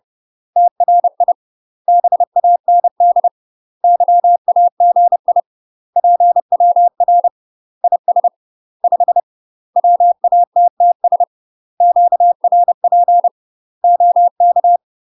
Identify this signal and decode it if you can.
Morse code, heard as a keyed tone, about 30 wpm, 700 Hz. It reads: TRI BAND YAGI PWR IS 5 WATTS QRP OK
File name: UK MASTER QSOs TRAINING / 30WPM / UK MASTER QSOs 0008